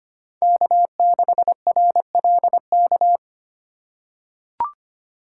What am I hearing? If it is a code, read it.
Morse code, heard as a keyed tone, about 25 wpm, 700 Hz. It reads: K6RLK
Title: USA Call Signs - 5 Characters - 4x Character Spacing - 25wpm